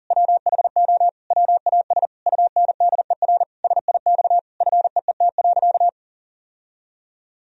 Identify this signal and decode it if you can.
Morse code, heard as a keyed tone, about 40 wpm, 700 Hz. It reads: WHO WAS UNDER SIX FEET.
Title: Sentences from Top 1000 Words - Review Part 2 - 40wpm